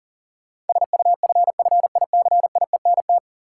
Morse code, heard as a keyed tone, about 40 wpm, 700 Hz. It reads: SUFFICIENT
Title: Sentences from Top 1400 Words - New Word Drill - 40wpm